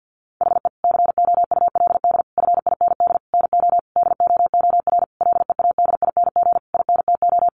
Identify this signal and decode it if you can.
Morse code, heard as a keyed tone, about 50 wpm, 700 Hz. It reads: HE COULD FIND NO DOOR LEADING INTO
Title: Sentences from Top 1100 Words 50wpm